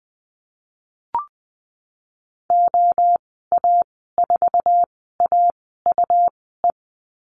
Morse code, heard as a keyed tone, about 20 wpm, spaced as for 15 wpm, 700 Hz. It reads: OA4AUE
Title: 154 Lesson 25g - Letter p - Call Signs 20wpm